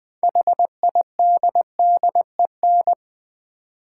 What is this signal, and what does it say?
Morse code, heard as a keyed tone, about 20 wpm, 700 Hz. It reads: HIDDEN
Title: 6 Letter Words - First 3 Letters Given - 20wpm